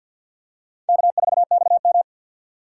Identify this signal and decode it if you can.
Morse code, heard as a keyed tone, about 51 wpm, 700 Hz. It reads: K4XM
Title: USA Call Signs - 6x Character Spacing - 50wpm